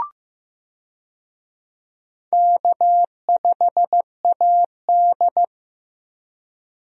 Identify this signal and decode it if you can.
Morse code, heard as a keyed tone, about 15 wpm, 700 Hz. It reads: K5AD
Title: USA Call Signs - 4 Characters - Repeated 2x - 15wpm